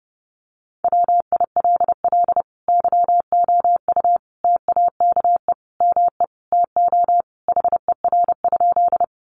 Morse code, heard as a keyed tone, about 30 wpm, 700 Hz. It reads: WILL YOU TAKE ME TO HER?
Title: Sentences from Top 200 Words 30wpm